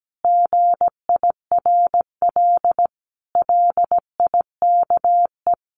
Morse code, heard as a keyed tone, about 17 wpm, 700 Hz. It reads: GIRL LIKE
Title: Sentences from Top 1200 Words 17wpm